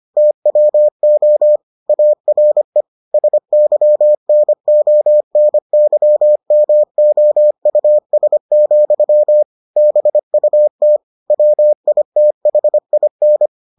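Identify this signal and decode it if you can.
Morse code, heard as a keyed tone, about 25 wpm, 600 Hz. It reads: TWO ARE SYNONYMOUS, BUT WITHIN
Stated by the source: Morse edition of the Foundations of Amateur Radio episodes.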